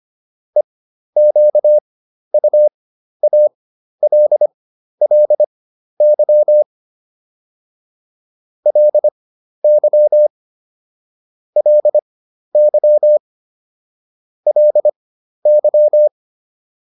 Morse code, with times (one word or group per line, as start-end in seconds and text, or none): 0.6-6.6 EQUALLY
8.7-10.3 LY
11.6-13.2 LY
14.5-16.1 LY